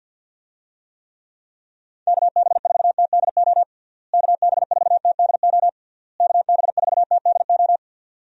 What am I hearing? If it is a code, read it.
Morse code, heard as a keyed tone, about 53 wpm, 700 Hz. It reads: KB4TDO KB4TDO KB4TDO
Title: USA Call Signs - 6 Characters - Repeated 3x - 50wpm